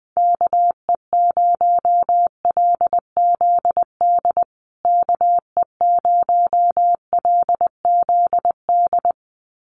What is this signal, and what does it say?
Morse code, heard as a keyed tone, about 20 wpm, 700 Hz. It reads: KE0LZD KE0LZD
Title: USA Call Signs Repeated 2x 20wpm